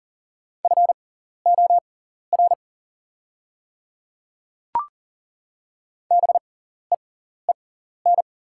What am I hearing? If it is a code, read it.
Morse code, heard as a keyed tone, about 40 wpm, spaced as for 14 wpm, 690 Hz. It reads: FOR BEEN
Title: 6x Character Spacing - Top 100 Words - 40wpm